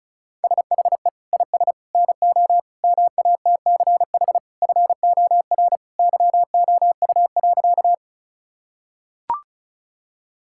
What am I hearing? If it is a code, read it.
Morse code, heard as a keyed tone, about 36 wpm, 700 Hz. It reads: SHE IS NO MATCH FOR YOU.
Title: Sentences from Top 1600 Words 35wpm